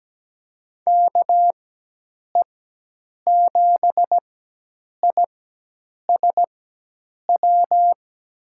Morse code, heard as a keyed tone, about 17 wpm, spaced as for 8 wpm, 700 Hz. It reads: KE7ISW
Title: USA Call Signs - 6 Characters - 4x Character Spacing - 17wpm